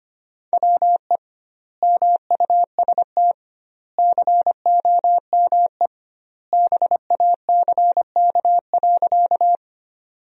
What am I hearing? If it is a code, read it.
Morse code, heard as a keyed tone, about 25 wpm, 700 Hz. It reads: WE MUST COME BACK.
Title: 2x Word-Spacing - Sentences from Top 200 Words - 25wpm